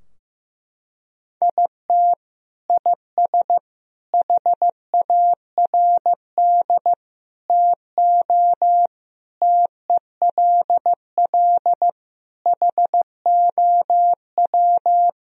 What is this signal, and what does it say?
Morse code, heard as a keyed tone, about 15 wpm, 700 Hz. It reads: IT IS HARD TO TELL HOW
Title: Sentences from Top 1000 Words - Review Part 3 - 15wpm